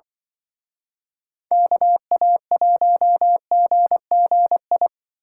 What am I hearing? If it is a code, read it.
Morse code, heard as a keyed tone, about 24 wpm, 700 Hz. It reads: KA1GGI
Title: Speed Racing - USA 2x3 Call Signs 20wpm